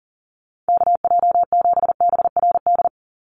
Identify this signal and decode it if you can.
Morse code, heard as a keyed tone, about 40 wpm, 700 Hz. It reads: KJ7BRD
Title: USA Call Signs 40wpm